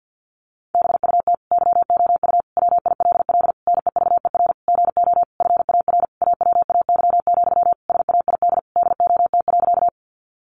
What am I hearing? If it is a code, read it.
Morse code, heard as a keyed tone, about 50 wpm, 700 Hz. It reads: BUT YOU WILL NEVER GO FAR AWAY, SAID DOT.
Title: Sentences from Top 1500 Words 50wpm